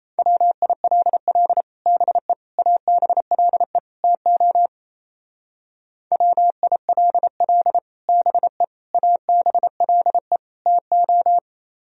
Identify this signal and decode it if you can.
Morse code, heard as a keyed tone, about 34 wpm, 700 Hz. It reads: WILL BE ABLE TO WILL BE ABLE TO
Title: Speed-Racing - Sets of 4 Words 28wpm